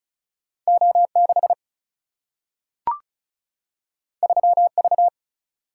Morse code, heard as a keyed tone, about 36 wpm, 700 Hz. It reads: O 6 3 V
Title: Single-Letter Single-Number 35wpm